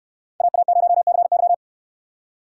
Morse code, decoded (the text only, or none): AA0CY